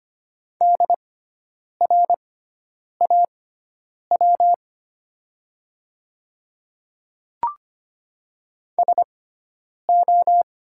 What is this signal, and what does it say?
Morse code, heard as a keyed tone, about 25 wpm, spaced as for 9 wpm, 700 Hz. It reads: DRAW SO
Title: 6x Character Spacing - Top 300 Words - 25wpm